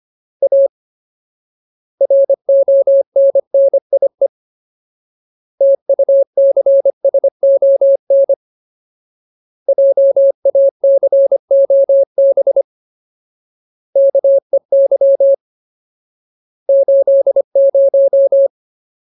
Morse code, heard as a keyed tone, about 25 wpm, 550 Hz. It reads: A RONNIE TUCSON JACOB KEY 80
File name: Ultimate Word List / 25WPM Ultimate